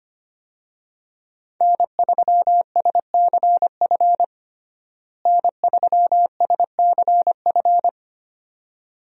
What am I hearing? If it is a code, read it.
Morse code, heard as a keyed tone, about 25 wpm, 700 Hz. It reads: N3SCF N3SCF
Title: USA Call Signs - 5 Characters - Repeated 2x - 25wpm